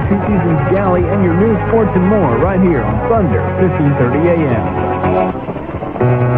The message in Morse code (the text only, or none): none